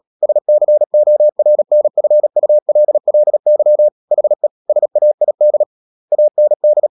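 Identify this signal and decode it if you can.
Morse code, heard as a keyed tone, about 37 wpm, 600 Hz. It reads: SCORNFULLY HE SAID AND
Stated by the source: Morse code practice at 35–39 wpm sending English text of Ken Follett's novel Never, from CW med Gnister.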